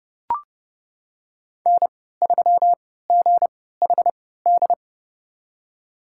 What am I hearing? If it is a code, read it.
Morse code, heard as a keyed tone, about 32 wpm, spaced as for 17 wpm, 700 Hz. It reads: N3GHD
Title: USA Call Signs - 5 Characters - 3x Character Spacing - 30wpm